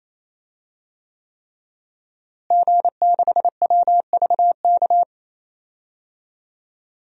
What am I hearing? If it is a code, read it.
Morse code, heard as a keyed tone, about 28 wpm, 700 Hz. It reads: G6WVK
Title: DXCC-100 Call Signs - 5 Characters - Repeated 3x - 28wpm